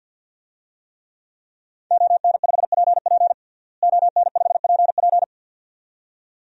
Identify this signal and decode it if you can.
Morse code, heard as a keyed tone, about 52 wpm, 700 Hz. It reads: ON5PP ON5PP
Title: DXCC-100 Call Signs - 5 Characters - Repeated 2x - 50wpm